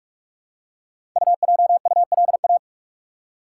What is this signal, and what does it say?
Morse code, heard as a keyed tone, about 44 wpm, 700 Hz. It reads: UJULA
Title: Code Groups - 5 Letters - 45wpm